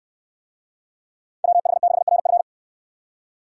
Morse code, heard as a keyed tone, about 69 wpm, 700 Hz. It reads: WH6RF